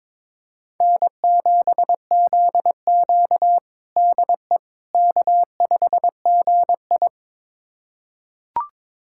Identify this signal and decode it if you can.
Morse code, heard as a keyed tone, about 22 wpm, 700 Hz. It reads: N7ZQ DE K5GI
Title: USA Callsign DE Callsign - 4 Characters - 22wpm